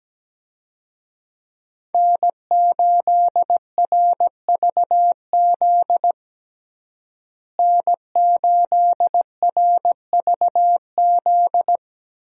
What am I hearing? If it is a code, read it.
Morse code, heard as a keyed tone, about 17 wpm, 700 Hz. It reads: N8RVZ N8RVZ
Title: USA Call Signs - 5 Characters - Repeated 2x - 17wpm